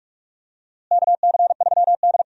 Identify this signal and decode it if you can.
Morse code, heard as a keyed tone, about 45 wpm, 700 Hz. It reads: KC3D
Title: USA Call Signs - 4 Characters - Repeated 3x - 45wpm